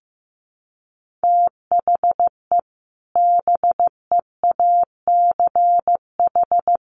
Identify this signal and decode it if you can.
Morse code, heard as a keyed tone, about 15 wpm, 700 Hz. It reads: THE BEACH
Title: Sentences from Top 1500 Words 15wpm